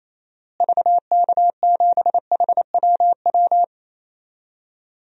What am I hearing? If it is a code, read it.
Morse code, heard as a keyed tone, about 27 wpm, 700 Hz. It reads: VK7HWW
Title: DXCC-100 Call Signs - 6 Characters - Repeated 3x - 28wpm